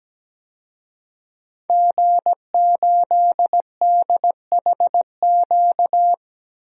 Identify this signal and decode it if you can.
Morse code, heard as a keyed tone, about 17 wpm, 700 Hz. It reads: G8DHQ
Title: DXCC-100 Call Signs - 5 Characters - 17wpm